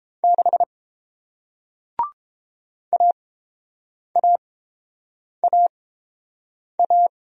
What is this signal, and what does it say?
Morse code, heard as a keyed tone, about 31 wpm, spaced as for 7 wpm, 700 Hz. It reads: 6 AAAA